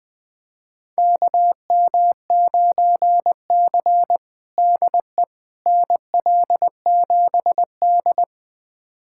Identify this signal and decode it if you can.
Morse code, heard as a keyed tone, about 21 wpm, 700 Hz. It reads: KM9C DE NL7D
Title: USA Callsign DE Callsign - 4 Characters - 20wpm